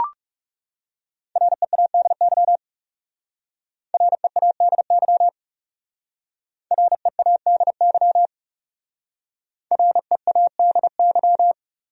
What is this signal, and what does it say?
Morse code, heard as a keyed tone, about 38 wpm, 700 Hz. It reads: READY READY READY READY
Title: Speed-Racing Top 500 Words 30wpm